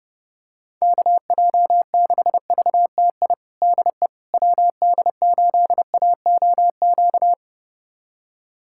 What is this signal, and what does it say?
Morse code, heard as a keyed tone, about 30 wpm, 700 Hz. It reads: KJ6VTI DE WD8AOQ
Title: USA Callsign DE Callsign - 6 Characters - 30wpm